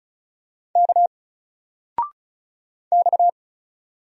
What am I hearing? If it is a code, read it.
Morse code, heard as a keyed tone, about 35 wpm, 700 Hz. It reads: K X
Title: Single Letter-Number 35wpm